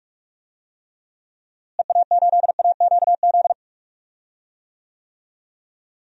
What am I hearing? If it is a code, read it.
Morse code, heard as a keyed tone, about 46 wpm, 700 Hz. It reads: EA8AQZ
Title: DXCC-100 Call Signs - 6 Characters - Repeated 2x - 45wpm